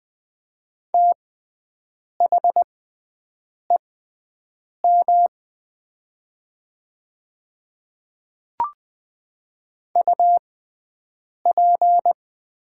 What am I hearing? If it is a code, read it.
Morse code, heard as a keyed tone, about 20 wpm, spaced as for 7 wpm, 700 Hz. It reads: THEM UP
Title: 6x Character Spacing - Top 100 Words - 20wpm